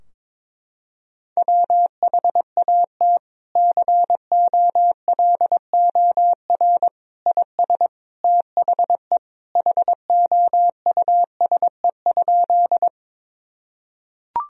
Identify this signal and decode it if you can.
Morse code, heard as a keyed tone, about 22 wpm, 700 Hz. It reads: WHAT COLOR IS THE HOUSE?
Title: Sentences from Top 1000 Words - Review Part 2 - 22wpm